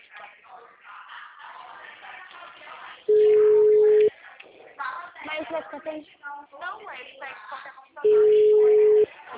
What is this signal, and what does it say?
Ringback tone (Europe, 425 Hz), with speech.
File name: out- -2017- - - .53